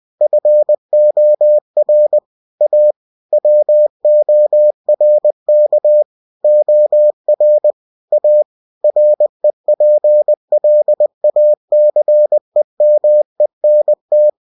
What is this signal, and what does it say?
Morse code, heard as a keyed tone, about 20 wpm, 600 Hz. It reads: FOR A WORK OR A REPLACEMENT
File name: cw-books / winnie-the-pooh / winnie-20wpm-page-0117